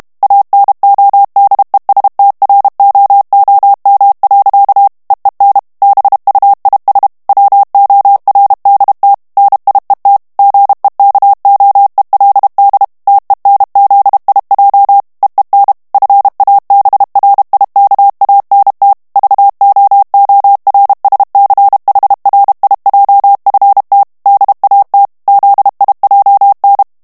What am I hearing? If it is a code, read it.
Morse code, heard as a keyed tone, about 32 wpm, 800 Hz. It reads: ANODESTROOM. EEN BUIS WORDT NIET GEKOELD TENZIJ EEN FABRIKANT VOORSCHRIJFT DAT ZIJN
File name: HOLLANDAIS texte clair 2 vitesse 30 mots minute